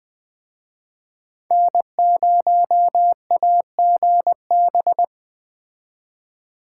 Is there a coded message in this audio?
Yes: Morse code from 1.5 to 5.0 s, heard as a keyed tone, about 20 wpm, 700 Hz. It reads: N0AGB